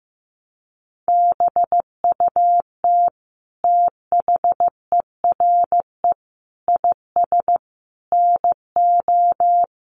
Morse code, heard as a keyed tone, about 15 wpm, 700 Hz. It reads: BUT THERE IS NO
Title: Sentences from Top 1300 Words 15wpm